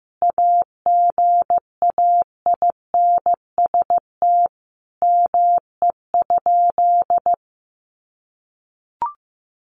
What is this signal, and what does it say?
Morse code, heard as a keyed tone, about 15 wpm, 700 Hz. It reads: AGAINST ME?
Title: Sentences from Top 1400 Words 15wpm